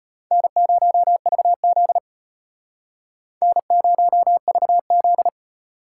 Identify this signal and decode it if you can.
Morse code, heard as a keyed tone, about 32 wpm, 700 Hz. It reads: N0VZ N0VZ